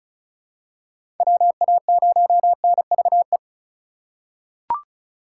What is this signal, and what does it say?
Morse code, heard as a keyed tone, about 36 wpm, 700 Hz. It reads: WA0NVE
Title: USA Call Signs - 6 Characters - 4x Character Spacing - 35wpm